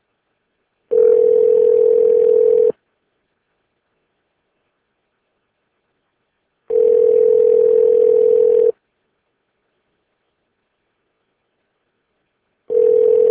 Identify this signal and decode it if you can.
Ringback tone (North America), 440 + 480 Hz.